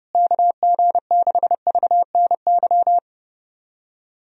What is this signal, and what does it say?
Morse code, heard as a keyed tone, about 32 wpm, 700 Hz. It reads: KG6VNY
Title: USA Call Signs - 6 Characters - Repeated 3x - 30wpm